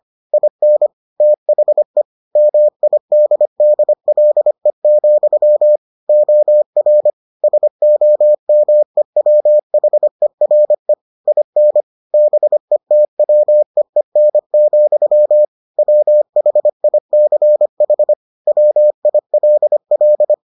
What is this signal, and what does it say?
Morse code, heard as a keyed tone, about 25 wpm, 600 Hz. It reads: IN THE MIDDLE, OR SOMEWHERE IN BETWEEN, WHICH WILL